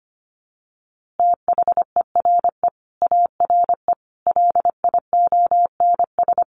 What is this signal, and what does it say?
Morse code, heard as a keyed tone, about 25 wpm, 700 Hz. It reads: THERE ARE LIONS
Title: Easy Phrases 25wpm